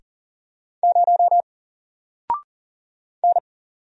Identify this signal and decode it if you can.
Morse code, heard as a keyed tone, about 40 wpm, 700 Hz. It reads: 0 N
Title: Single Character (Letter:Number:Punctuation:Prosign) Alt 40wpm